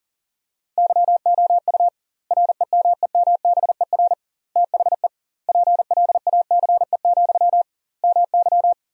Morse code, heard as a keyed tone, about 40 wpm, 700 Hz. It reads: YOU REMEMBER THE PLACE, MY
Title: 2x Word-Spacing - Sentences from Top 400 Words - 40wpm